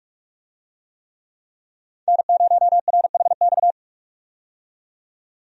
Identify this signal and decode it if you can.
Morse code, heard as a keyed tone, about 41 wpm, 700 Hz. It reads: N0RHX
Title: USA Call Signs - 5 Characters - 45wpm